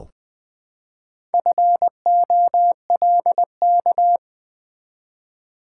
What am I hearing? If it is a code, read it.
Morse code, heard as a keyed tone, about 20 wpm, 700 Hz. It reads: FOLK
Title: Build A Word - 1 Letter At A Time - 3 to 5-Letter Words - 20wpm